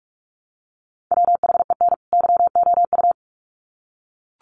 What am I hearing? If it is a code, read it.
Morse code, heard as a keyed tone, about 44 wpm, 700 Hz. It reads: WHEN YOU
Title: Sets of 2 Words 45wpm